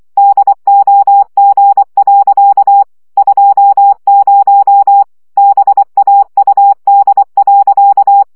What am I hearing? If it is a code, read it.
Morse code, heard as a keyed tone, about 24 wpm, 800 Hz. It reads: DOG. 20 BAUD.
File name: CW 20 baud